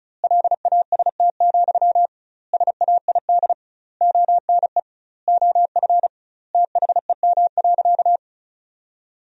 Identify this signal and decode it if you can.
Morse code, heard as a keyed tone, about 36 wpm, 700 Hz. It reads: LAST, SAID ONE OF THEM.
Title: 2x Word-Spacing - Sentences from Top 300 Words - 35wpm